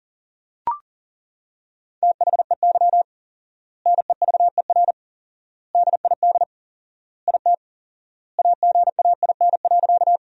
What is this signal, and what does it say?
Morse code, heard as a keyed tone, about 40 wpm, 700 Hz. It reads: THEY NEVER DID IT AGAIN.
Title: 4x Word-Spacing - Sentences from Top 300 Words - 40wpm